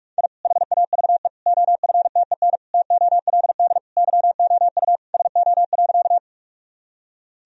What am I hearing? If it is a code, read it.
Morse code, heard as a keyed tone, about 46 wpm, 700 Hz. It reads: I HAVE OFTEN TOLD YOU SO.
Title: Sentences from Top 1000 Words - Review Part 1 - 45wpm